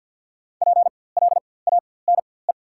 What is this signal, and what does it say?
Morse code, heard as a keyed tone, about 54 wpm, spaced as for 24 wpm, 690 Hz. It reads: PLANE
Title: 4x Character Spacing - Top 500 Words - 50wpm